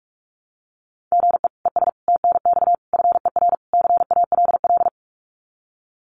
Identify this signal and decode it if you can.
Morse code, heard as a keyed tone, about 46 wpm, 700 Hz. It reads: GE ES TNX FER CALL
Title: Common QSO Phrases Encore 45wpm